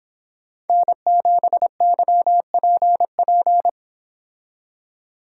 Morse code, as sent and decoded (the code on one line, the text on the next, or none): -. --... -.-- .--. .--.
N7YPP